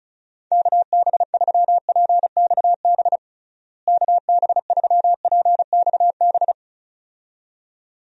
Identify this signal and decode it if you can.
Morse code, heard as a keyed tone, about 34 wpm, 700 Hz. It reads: KB3PXB KB3PXB
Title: USA Call Signs - 6 Characters - Repeated 2x - 35wpm